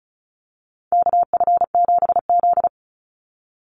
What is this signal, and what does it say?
Morse code, heard as a keyed tone, about 34 wpm, 700 Hz. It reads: KF7Z
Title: Call Signs Repeated 3x - 35wpm